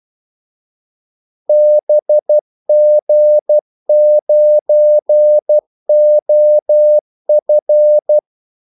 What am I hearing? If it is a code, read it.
Morse code, heard as a keyed tone, about 12 wpm, 600 Hz. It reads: BG9OF
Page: 2 ekstra ordmellemrum mellem kaldesignalerne